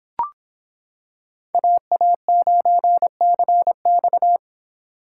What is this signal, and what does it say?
Morse code, heard as a keyed tone, about 25 wpm, 700 Hz. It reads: AA9CX